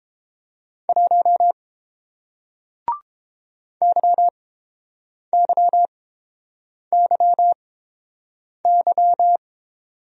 Morse code, heard as a keyed tone, about 30 wpm, spaced as for 7 wpm, 700 Hz. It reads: 1 YYYY